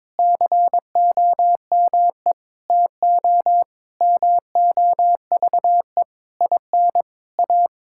Morse code, heard as a keyed tone, about 22 wpm, 700 Hz. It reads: COME TO MOVE IN A
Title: Sentences from Top 1000 Words - Review Part 5 - 22wpm